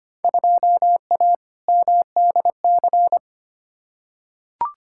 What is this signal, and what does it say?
Morse code, heard as a keyed tone, about 25 wpm, 700 Hz. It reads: 2A MDC
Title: Field Day Exchanges Encore - 25wpm